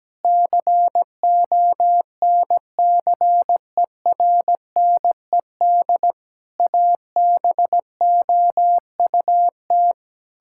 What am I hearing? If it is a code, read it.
Morse code, heard as a keyed tone, about 17 wpm, 700 Hz. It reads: CONCERNED ABOUT
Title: Sentences from Top 1000 Words - Review Part 6 - 17wpm